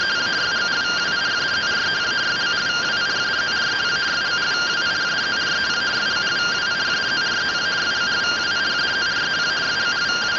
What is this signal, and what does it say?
Two-tone teleprinter signal, about 170 Hz shift.